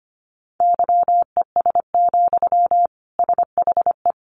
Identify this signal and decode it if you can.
Morse code, heard as a keyed tone, about 25 wpm, 700 Hz. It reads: YES, SHE
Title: Sentences from Top 700 Words 25wpm